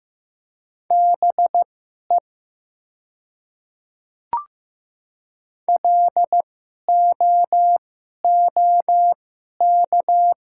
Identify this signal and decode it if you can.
Morse code, heard as a keyed tone, about 15 wpm, spaced as for 11 wpm, 700 Hz. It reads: BE LOOK
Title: 2x Character Spacing - Top 100 Words - 15wpm